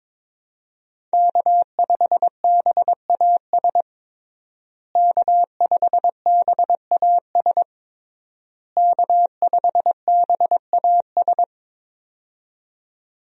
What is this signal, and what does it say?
Morse code, heard as a keyed tone, about 22 wpm, 700 Hz. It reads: K5BAS K5BAS K5BAS
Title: USA Call Signs - 5 Characters - Repeated 3x - 22wpm